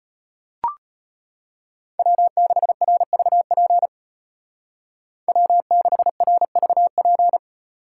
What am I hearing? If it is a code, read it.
Morse code, heard as a keyed tone, about 31 wpm, 700 Hz. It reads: W6RVP W6RVP
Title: Speed Racing - USA Call Signs 25wpm